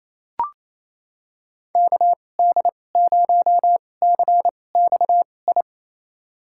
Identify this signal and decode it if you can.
Morse code, heard as a keyed tone, about 28 wpm, spaced as for 20 wpm, 700 Hz. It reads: KD0CXI
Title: USA Call Signs - 2x Character Spacing - 28wpm